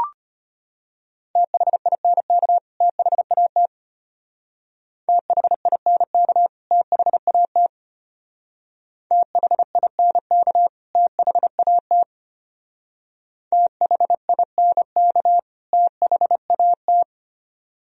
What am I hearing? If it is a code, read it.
Morse code, heard as a keyed tone, about 31 wpm, 700 Hz. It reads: THINK THAT THINK THAT THINK THAT THINK THAT